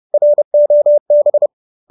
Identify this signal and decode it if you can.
Morse code, heard as a keyed tone, about 30 wpm, 570 Hz. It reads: ROB